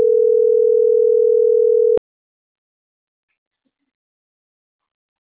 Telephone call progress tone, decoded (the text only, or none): ringback tone